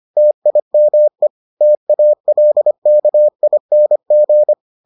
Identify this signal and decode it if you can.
Morse code, heard as a keyed tone, about 25 wpm, 600 Hz. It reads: TIME TALKING